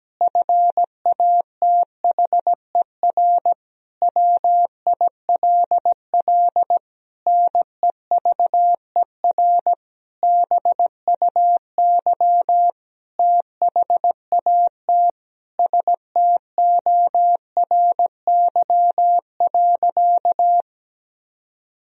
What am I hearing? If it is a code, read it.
Morse code, heard as a keyed tone, about 17 wpm, 700 Hz. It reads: FATHER WILL NEVER BUY THAT STORY.
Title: Sentences from Top 1000 Words - Review Part 4 - 17wpm